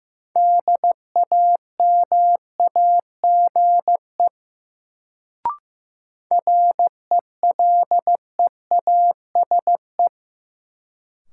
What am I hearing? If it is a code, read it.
Morse code, heard as a keyed tone, about 15 wpm, 700 Hz. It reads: DAMAGE RELEASE
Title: Sentences from Top 1100 Words - New Word Drill - 15wpm